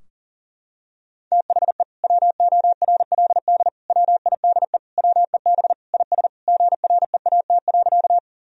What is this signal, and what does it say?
Morse code, heard as a keyed tone, about 40 wpm, 700 Hz. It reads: THE WORLD WIDE WEB IS GREAT.
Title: Sentences from Top 1000 Words - Review Part 4 - 40wpm